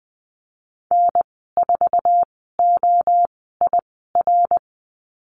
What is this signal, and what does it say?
Morse code, heard as a keyed tone, about 20 wpm, spaced as for 15 wpm, 700 Hz. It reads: N4OIR
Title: 062 Lesson 11g - Letter h - Call Signs 20wpm